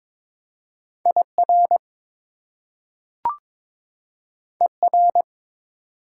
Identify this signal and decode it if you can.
Morse code, heard as a keyed tone, about 22 wpm, 700 Hz. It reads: IR ER
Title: Prefixes and Suffixes 22wpm